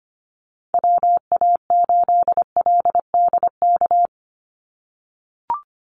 Morse code, heard as a keyed tone, about 25 wpm, 690 Hz. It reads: WA8LDK
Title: USA Call Signs 25wpm